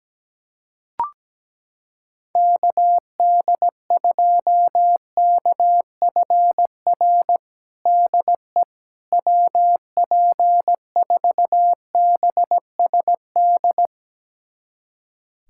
Morse code, heard as a keyed tone, about 17 wpm, 700 Hz. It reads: KD2KFR DE WP4BSD
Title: USA Callsign DE Callsign 17wpm